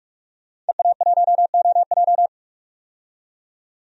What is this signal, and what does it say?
Morse code, heard as a keyed tone, about 48 wpm, 700 Hz. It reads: EA1OJ
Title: DXCC-100 Call Signs - 5 Characters - 45wpm